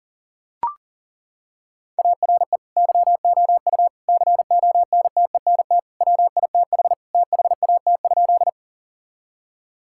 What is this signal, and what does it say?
Morse code, heard as a keyed tone, about 37 wpm, 700 Hz. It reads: ARE YOU CONTENT WITH THAT?
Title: Sentences from Top 1000 Words - Review Part 6 - 40wpm